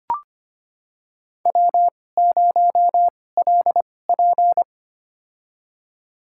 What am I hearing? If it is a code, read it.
Morse code, heard as a keyed tone, about 25 wpm, spaced as for 18 wpm, 700 Hz. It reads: W0LP
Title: USA Call Signs - 4 Characters - 2x Character Spacing - 25wpm